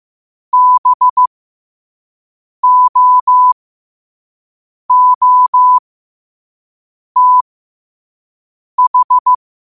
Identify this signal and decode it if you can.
Morse code, heard as a keyed tone, about 15 wpm, spaced as for 5 wpm, 1000 Hz. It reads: BOOTH
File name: qotdfiles / 5wpm